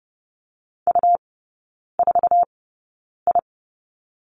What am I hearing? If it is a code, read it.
Morse code, heard as a keyed tone, about 30 wpm, spaced as for 9 wpm, 700 Hz. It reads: U4I
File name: Lesson 17e - Letter u - Three Characters 30wpm